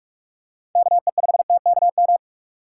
Morse code, heard as a keyed tone, about 44 wpm, 700 Hz. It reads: KE5TKM
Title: USA Call Signs - 6 Characters - 6x Character Spacing - 45wpm